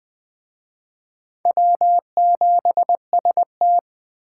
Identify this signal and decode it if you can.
Morse code, heard as a keyed tone, about 24 wpm, 700 Hz. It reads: W7ST
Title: Speed Racing - USA Call Signs 20wpm